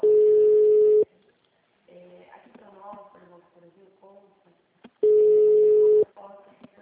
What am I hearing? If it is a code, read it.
Ringback tone (Europe, 425 Hz), with speech.